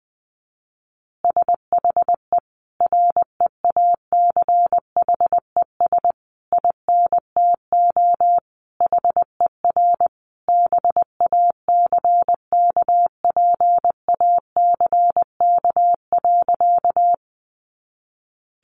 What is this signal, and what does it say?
Morse code, heard as a keyed tone, about 20 wpm, 700 Hz. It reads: SHE REACHES INTO HER BACKPACK.
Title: Additional Easy Phrases 20wpm